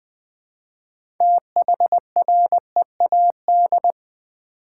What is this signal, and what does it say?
Morse code, heard as a keyed tone, about 20 wpm, 700 Hz. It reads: THREAD
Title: Build A Word - 2 Letters At A Time - 6-Letter Words - 20wpm